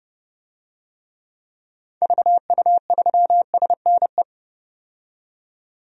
Morse code, heard as a keyed tone, about 31 wpm, 700 Hz. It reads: VU3SNE
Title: DXCC-100 Call Signs - 6 Characters - 30wpm